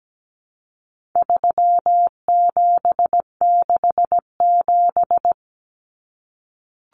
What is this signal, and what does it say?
Morse code, heard as a keyed tone, about 17 wpm, 700 Hz. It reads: 3767
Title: Numbers - 4-Digit 17wpm